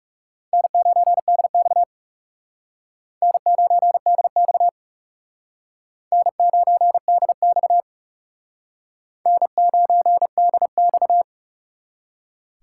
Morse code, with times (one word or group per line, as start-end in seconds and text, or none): 0.5-1.8 N9DX
3.2-4.7 N9DX
6.1-7.8 N9DX
9.3-11.2 N9DX